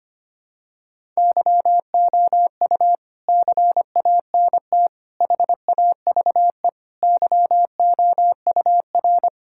Morse code, heard as a keyed tone, about 25 wpm, 700 Hz. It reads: YOU CANT HAVE YOUR
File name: Sets of 9 Words - 25wpm